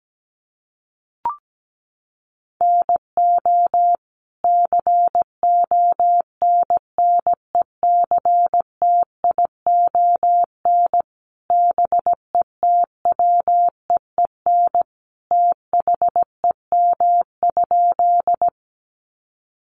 Morse code, heard as a keyed tone, about 17 wpm, 700 Hz. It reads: NO CONNECTION BETWEEN THEM?
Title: Sentences from Top 1300 Words 17wpm